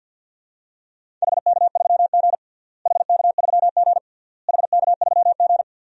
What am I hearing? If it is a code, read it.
Morse code, heard as a keyed tone, about 49 wpm, 700 Hz. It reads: HK3G HK3G HK3G
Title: DXCC-100 Call Signs - 4 Characters - Repeated 3x - 50wpm